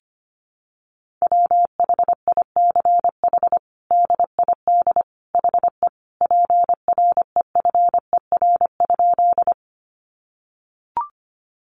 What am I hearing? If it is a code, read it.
Morse code, heard as a keyed tone, about 25 wpm, 700 Hz. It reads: WHICH DID HE PREFER?
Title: Sentences from Top 1100 Words 25wpm